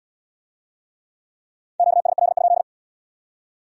Morse code, heard as a keyed tone, about 72 wpm, 700 Hz. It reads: QSL?